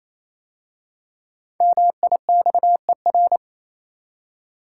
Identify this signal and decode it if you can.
Morse code, heard as a keyed tone, about 27 wpm, 700 Hz. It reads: MIXER
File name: Easy Words - Speed-Racing with Hints - 22wpm